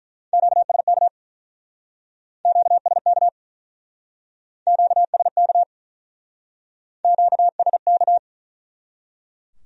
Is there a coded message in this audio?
Yes: Morse code from 0.3 to 8.2 s, heard as a keyed tone, about 47 wpm, 700 Hz. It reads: QSK QSK QSK QSK